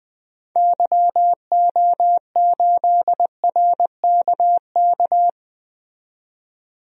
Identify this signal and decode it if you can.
Morse code, heard as a keyed tone, about 20 wpm, 700 Hz. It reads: YO8RKK
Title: DXCC-100 Call Signs - 6 Characters - 20wpm